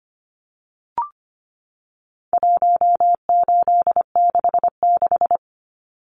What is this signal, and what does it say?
Morse code, heard as a keyed tone, about 25 wpm, 700 Hz. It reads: 1866
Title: Numbers - 4-Digit 25wpm